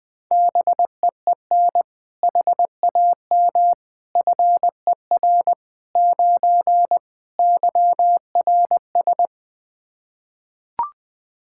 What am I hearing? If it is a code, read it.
Morse code, heard as a keyed tone, about 20 wpm, 700 Hz. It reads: BEEN HAM FER 9 YRS
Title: Practice QSO - Exchange 3 - More Info - 20wpm